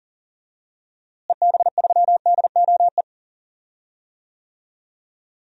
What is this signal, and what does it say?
Morse code, heard as a keyed tone, about 40 wpm, 700 Hz. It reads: EB3DOE
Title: DXCC-100 Call Signs - 6 Characters - Repeated 3x - 40wpm